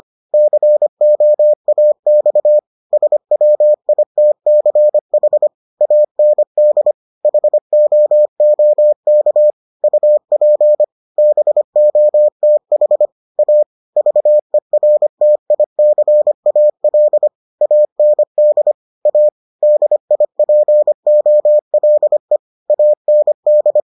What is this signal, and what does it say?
Morse code, heard as a keyed tone, about 25 wpm, 600 Hz. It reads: COAX SWITCH AND HOOK UP BOTH A VERTICAL AND A DIPOLE AND